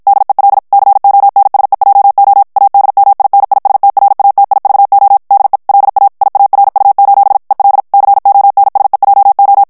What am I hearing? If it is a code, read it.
Morse code, heard as a keyed tone, about 53 wpm, 800 Hz. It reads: DEL CONSEJO ADMINISTRATIVO DE LA IARU: EL CONSEJO
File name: ESPAGNOL TEXTE EN CLAIR 1 VITESSE 50 MOTS MINUTES